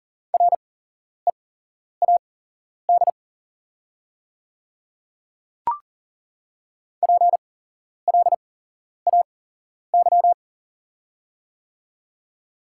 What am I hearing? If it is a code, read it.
Morse code, heard as a keyed tone, about 40 wpm, spaced as for 11 wpm, 690 Hz. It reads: READ PLAY
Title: 8x Character Spacing - Top 200 Words - 40wpm